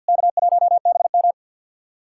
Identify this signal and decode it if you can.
Morse code, heard as a keyed tone, about 50 wpm, 700 Hz. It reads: K1BM
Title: SOTA QSO Practice 50wpm